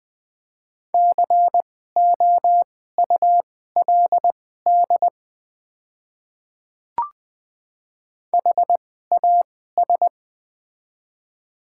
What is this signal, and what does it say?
Morse code, heard as a keyed tone, about 20 wpm, spaced as for 14 wpm, 700 Hz. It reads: COULD HAS
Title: 2x Character Spacing - Top 100 Words - 20wpm